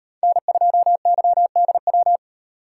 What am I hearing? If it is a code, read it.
Morse code, heard as a keyed tone, about 38 wpm, 700 Hz. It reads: N2YDW